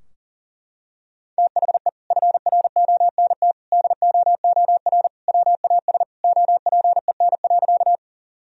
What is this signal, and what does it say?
Morse code, heard as a keyed tone, about 40 wpm, 700 Hz. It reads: THE FRONT DOOR WAS OPEN.
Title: Sentences from Top 1000 Words - Review Part 2 - 40wpm